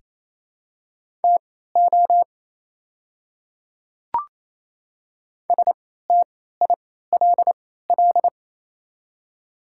Morse code, heard as a keyed tone, about 28 wpm, spaced as for 16 wpm, 700 Hz. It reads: TO STILL
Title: 3x Character Spacing - Top 500 Words - 28wpm